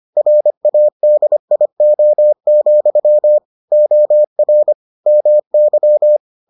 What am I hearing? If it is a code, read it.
Morse code, heard as a keyed tone, about 25 wpm, 600 Hz. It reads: RADIO, OR MY